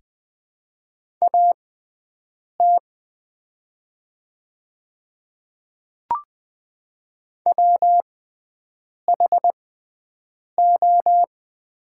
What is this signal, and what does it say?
Morse code, heard as a keyed tone, about 20 wpm, spaced as for 7 wpm, 700 Hz. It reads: AT WHO
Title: 6x Character Spacing - Top 100 Words - 20wpm